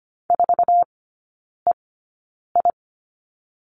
Morse code, heard as a keyed tone, about 25 wpm, spaced as for 9 wpm, 700 Hz. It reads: 4EI
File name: Lesson 14e - Number 2 - Three Characters 25wpm